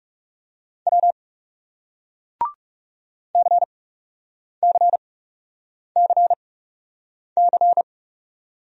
Morse code, heard as a keyed tone, about 40 wpm, spaced as for 8 wpm, 700 Hz. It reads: W CCCC